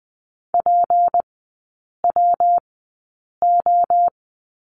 Morse code, heard as a keyed tone, about 20 wpm, spaced as for 8 wpm, 700 Hz. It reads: PWO
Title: 152 Lesson 25e - Letter p - Three Characters 20wpm